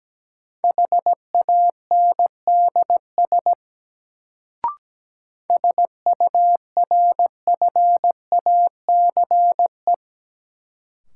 Morse code, heard as a keyed tone, about 17 wpm, 700 Hz. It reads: HANDS SURFACE
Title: Sets of 4 Words - New Word Drill - 17wpm